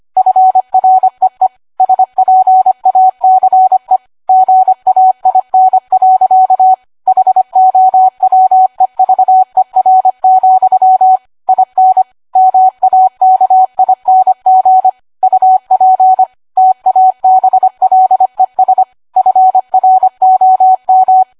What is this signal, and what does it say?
Morse code, heard as a keyed tone, about 25 wpm, 750 Hz, with radio noise. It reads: FREE SPACE GAIN. HOWEVER, IN MAKING UP TABLES FROM